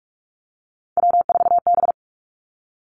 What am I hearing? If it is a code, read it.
Morse code, heard as a keyed tone, about 46 wpm, 700 Hz. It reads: W4B
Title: Three Challenging Letter-Number Combinations 45wpm